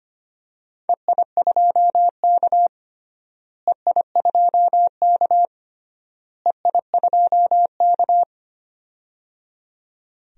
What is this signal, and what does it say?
Morse code, heard as a keyed tone, about 25 wpm, 700 Hz. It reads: EI2K EI2K EI2K
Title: DXCC-100 Call Signs - 4 Characters - Repeated 3x - 25wpm